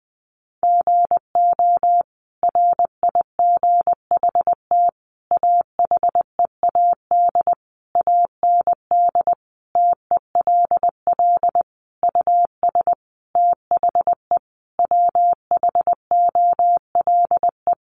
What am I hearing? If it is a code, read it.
Morse code, heard as a keyed tone, about 20 wpm, 700 Hz. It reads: GO RIGHT AHEAD AND TELL US THE WHOLE
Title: Sentences from Top 1100 Words 20wpm